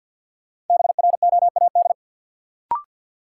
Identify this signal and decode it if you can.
Morse code, heard as a keyed tone, about 50 wpm, 700 Hz. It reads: BROAD
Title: Sentences from Top 1000 Words - New Word Drill - 50wpm